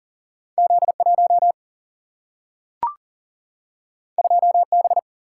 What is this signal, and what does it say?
Morse code, heard as a keyed tone, about 41 wpm, 700 Hz. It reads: Z1 2B
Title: Single-Letter Single-Number 40wpm